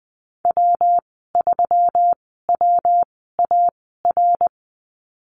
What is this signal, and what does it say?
Morse code, heard as a keyed tone, about 20 wpm, spaced as for 15 wpm, 700 Hz. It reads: W3WAR
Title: 167 Lesson 27g - Number 7 - Call Signs 20wpm